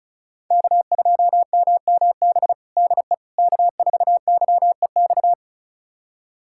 Morse code, heard as a keyed tone, about 34 wpm, 700 Hz. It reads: K2MMB DE K4YEX
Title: USA Callsign DE Callsign - 5 Characters - 35wpm